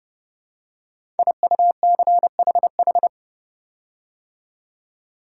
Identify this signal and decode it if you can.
Morse code, heard as a keyed tone, about 30 wpm, 700 Hz. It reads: IUCHH
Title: Code Groups - 5 Letters - 30wpm